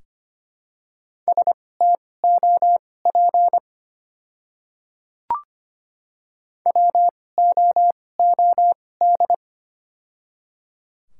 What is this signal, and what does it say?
Morse code, heard as a keyed tone, about 25 wpm, spaced as for 18 wpm, 700 Hz. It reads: STOP WOOD
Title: 2x Character Spacing - Top 300 Words - 25wpm